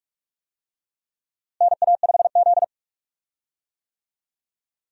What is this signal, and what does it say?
Morse code, heard as a keyed tone, about 47 wpm, 700 Hz. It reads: NA5Z
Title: USA Call Signs - 4 Characters - Repeated 2x - 45wpm